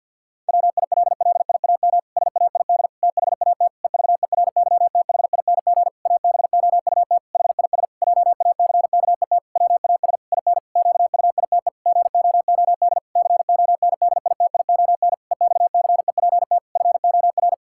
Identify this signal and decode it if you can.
Morse code, heard as a keyed tone, about 52 wpm, 700 Hz. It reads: WILLIAM SAID THAT EVERYTHING ABOUT HIS JACKET WAS IN QUITE GOOD CONDITION EXCEPT FOR
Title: Pangram Challenge v2 - 50wpm